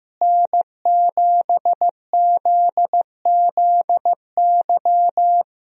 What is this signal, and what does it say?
Morse code, heard as a keyed tone, about 15 wpm, 700 Hz. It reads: N7ZZY
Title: USA Call Signs - 5 Characters - Repeated 3x - 15wpm